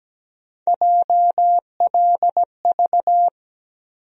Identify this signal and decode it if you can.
Morse code, heard as a keyed tone, about 17 wpm, 700 Hz. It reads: JLV
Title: USA Call Sign Postfixes - 3 Letters 17wpm